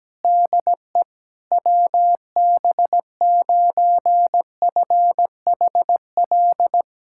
Morse code, heard as a keyed tone, about 17 wpm, 700 Hz. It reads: DE WB9FHL
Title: USA Callsign DE Callsign - 6 Characters - 17wpm